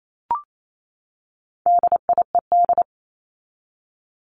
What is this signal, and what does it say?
Morse code, heard as a keyed tone, about 28 wpm, 700 Hz. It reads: DIED
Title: ED Suffix - Top 100 Words with Base Words 28wpm